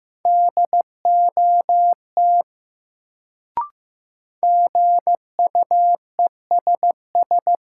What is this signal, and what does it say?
Morse code, heard as a keyed tone, about 15 wpm, 700 Hz. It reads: DOT GUESS
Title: Sentences from Top 1500 Words - New Word Drill Alt - 15wpm